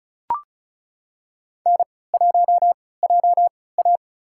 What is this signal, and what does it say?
Morse code, heard as a keyed tone, about 36 wpm, spaced as for 20 wpm, 700 Hz. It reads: N1JA
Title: USA Call Signs - 4 Characters - 3x Character Spacing - 35wpm